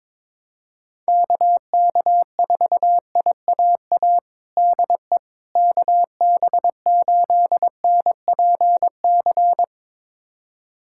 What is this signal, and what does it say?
Morse code, heard as a keyed tone, about 23 wpm, 700 Hz. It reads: KK4IAA DE KB8NPC
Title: USA Callsign DE Callsign - 6 Characters - 22wpm